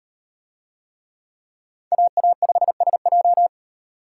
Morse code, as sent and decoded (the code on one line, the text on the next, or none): .- .- ..... ... .---
AA5SJ